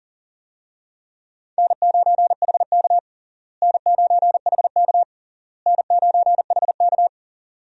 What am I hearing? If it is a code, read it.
Morse code, heard as a keyed tone, about 40 wpm, 700 Hz. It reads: N9HK N9HK N9HK
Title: USA Call Signs - 4 Characters - Repeated 3x - 40wpm